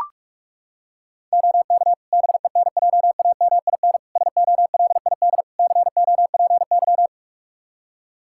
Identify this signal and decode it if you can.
Morse code, heard as a keyed tone, about 41 wpm, 700 Hz. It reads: OK BENJAMIN SOLID COPY
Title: K1USN SST QSO Practice 45wpm